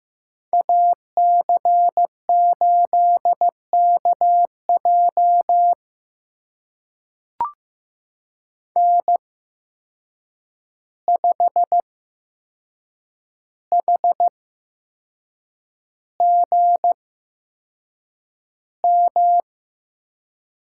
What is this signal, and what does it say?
Morse code, heard as a keyed tone, about 15 wpm, spaced as for 6 wpm, 700 Hz. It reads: AC8KJ N5HGM
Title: USA Call Signs - 5 Characters - 8x Character Spacing - 15wpm